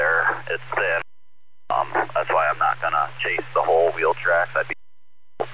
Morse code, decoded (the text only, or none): none